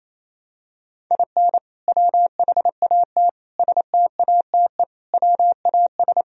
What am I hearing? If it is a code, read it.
Morse code, heard as a keyed tone, about 28 wpm, 700 Hz. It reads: IN WHAT STATE WAS
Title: Sentences from Top 1000 Words - Review Part 2 - 28wpm